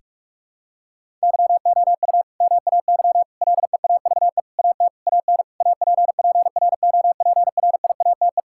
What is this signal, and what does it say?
Morse code, heard as a keyed tone, about 46 wpm, 700 Hz. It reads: YOU MAY LEAVE AT AN APPROPRIATE
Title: Sentences from Top 1000 Words - Review Part 5 - 45wpm